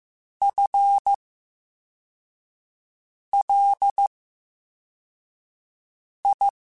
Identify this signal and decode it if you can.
Morse code, heard as a keyed tone, about 15 wpm, 800 Hz. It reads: F L I
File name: Special INGs-mp3s / Special INGs-mp3s / VerySlowToFaster Ing @13